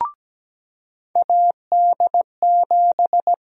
Morse code, heard as a keyed tone, about 17 wpm, 700 Hz. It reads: AD7
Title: USA Call Sign Prefixes - 17wpm